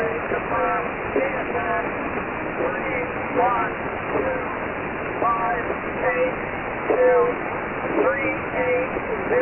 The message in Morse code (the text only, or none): none